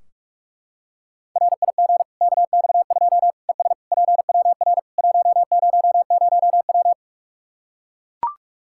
Morse code, heard as a keyed tone, about 42 wpm, 700 Hz. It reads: RIG KX2 ES PWR 100W
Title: Practice QSO - Exchange 2 - Info - 45wpm